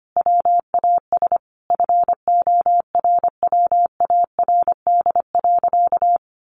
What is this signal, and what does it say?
Morse code, heard as a keyed tone, about 25 wpm, 700 Hz. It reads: WAS FORWARD.
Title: Sentences from Top 1500 Words 25wpm